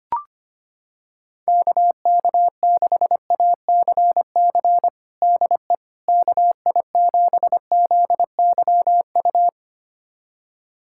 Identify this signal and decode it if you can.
Morse code, heard as a keyed tone, about 25 wpm, 700 Hz. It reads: KK6ACC DE KI7ZYU
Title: USA Callsign DE Callsign - 6 Characters - 25wpm